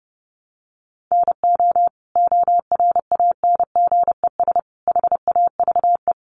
Morse code, heard as a keyed tone, about 30 wpm, 700 Hz. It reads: NO ORANGES HAVE
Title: Easy Phrases 30wpm